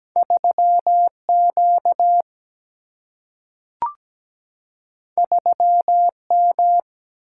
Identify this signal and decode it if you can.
Morse code, heard as a keyed tone, about 17 wpm, 700 Hz. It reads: 3Q 3M
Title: Single-Letter Single-Number 17wpm